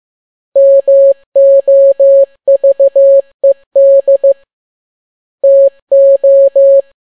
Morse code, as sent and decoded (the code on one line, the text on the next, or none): -- --- ...- . -.. / - ---
MOVED TO